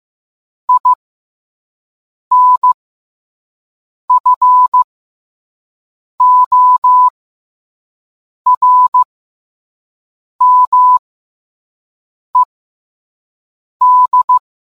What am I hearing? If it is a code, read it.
Morse code, heard as a keyed tone, about 15 wpm, spaced as for 5 wpm, 1000 Hz. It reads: INFORMED